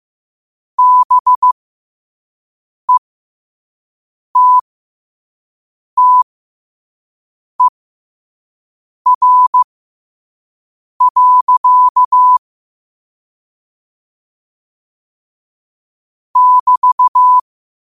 Morse code, heard as a keyed tone, about 15 wpm, spaced as for 5 wpm, 1000 Hz. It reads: BETTER. <BT>
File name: qotdfiles / 5wpm